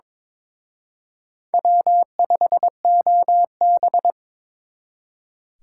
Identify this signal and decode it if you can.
Morse code, heard as a keyed tone, about 26 wpm, 700 Hz. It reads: W5OB
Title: Speed Racing - USA 1x2 Call Signs 22wpm